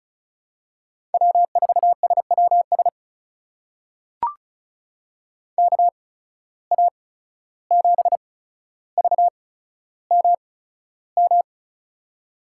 Morse code, heard as a keyed tone, about 34 wpm, spaced as for 14 wpm, 700 Hz. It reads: W4SWS KA7VMM
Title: USA Call Signs - 8x Character Spacing - 35wpm